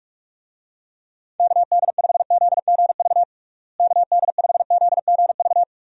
Morse code, heard as a keyed tone, about 45 wpm, 700 Hz. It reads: KD5ZGV KD5ZGV
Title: USA Call Signs - 6 Characters - Repeated 2x - 45wpm